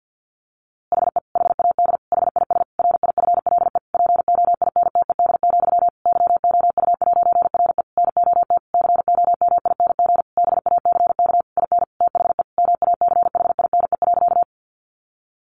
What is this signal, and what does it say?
Morse code, heard as a keyed tone, about 50 wpm, 700 Hz. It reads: HE HAD HIS RIFLE POINTED, YOU'RE NOT COMING BACK IN THE MACHINE.
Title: A Sound Of Thunder 50wpm